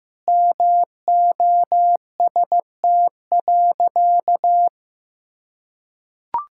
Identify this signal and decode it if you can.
Morse code, heard as a keyed tone, about 15 wpm, 700 Hz. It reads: MOST.
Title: Sentences from Top 1600 Words 15wpm